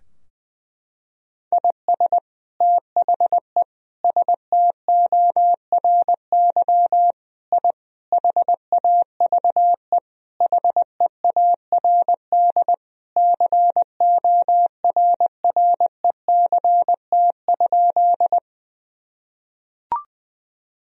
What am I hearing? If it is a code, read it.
Morse code, heard as a keyed tone, about 20 wpm, 700 Hz. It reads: IS THE STORY I HAVE HEARD CORRECT?
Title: Sentences from Top 1000 Words - Review Part 2 - 20wpm